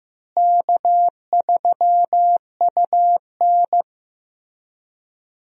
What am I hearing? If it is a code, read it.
Morse code, heard as a keyed tone, about 15 wpm, 700 Hz. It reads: K3UN
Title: USA Call Signs - 4 Characters - Repeated 2x - 15wpm